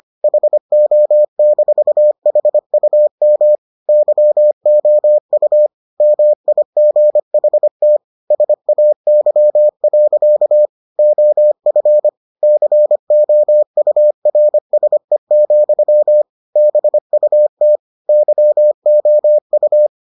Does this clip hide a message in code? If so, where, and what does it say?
Yes: Morse code from 0.2 to 19.9 s, heard as a keyed tone, about 25 wpm, 600 Hz. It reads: HO-HUM YOU MIGHT SAY. OF COURSE, BUT YOU